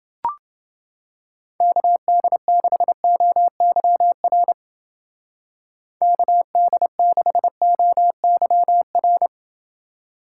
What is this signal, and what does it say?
Morse code, heard as a keyed tone, about 28 wpm, 700 Hz. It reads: KD6OYR KD6OYR